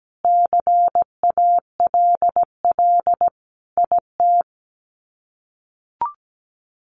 Morse code, heard as a keyed tone, about 17 wpm, 700 Hz. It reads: CALL IT
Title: Sets of 5 Words Encore2 17wpm